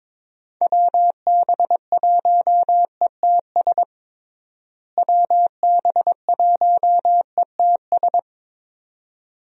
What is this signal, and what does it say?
Morse code, heard as a keyed tone, about 22 wpm, 700 Hz. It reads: WB1ETS WB1ETS
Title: USA Call Signs - 6 Characters - Repeated 2x - 22wpm